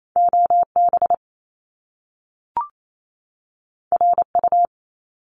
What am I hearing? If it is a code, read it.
Morse code, heard as a keyed tone, about 28 wpm, 700 Hz. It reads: OB RU
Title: Top Two Letter Combinations in Words 28wpm